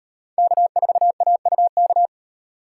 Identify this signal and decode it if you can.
Morse code, heard as a keyed tone, about 35 wpm, 700 Hz. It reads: K4AUK